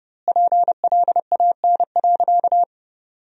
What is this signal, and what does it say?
Morse code, heard as a keyed tone, about 32 wpm, 700 Hz. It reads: PLAN.
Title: 2x Word-Spacing - Sentences from Top 500 Words - 30wpm